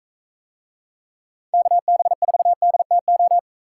Morse code, heard as a keyed tone, about 45 wpm, 700 Hz. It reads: KB4DTO